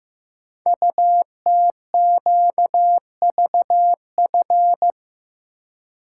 Morse code, heard as a keyed tone, about 15 wpm, 700 Hz. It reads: UTQVF
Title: Code Groups - 5 Letters - 15wpm